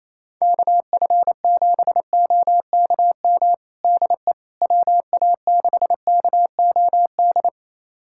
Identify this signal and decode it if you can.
Morse code, heard as a keyed tone, about 28 wpm, 700 Hz. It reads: KF7OKM DE WA6KOD
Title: USA Callsign DE Callsign - 6 Characters - 28wpm